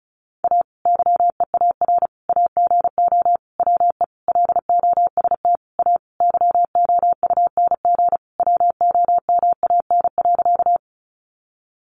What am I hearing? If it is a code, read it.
Morse code, heard as a keyed tone, about 34 wpm, 700 Hz. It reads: A YEAR AGO WE LOST A YOUNG WOMAN.
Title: Sentences from Top 1400 Words 35wpm